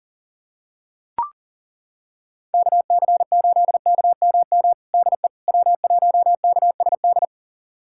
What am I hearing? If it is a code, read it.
Morse code, heard as a keyed tone, about 37 wpm, 700 Hz. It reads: KC8KMM DE W1KSD
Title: USA Callsign DE Callsign 40wpm